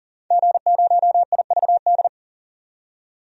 Morse code, heard as a keyed tone, about 40 wpm, 700 Hz. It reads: G0IVD
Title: DXCC-100 Call Signs - 5 Characters - 40wpm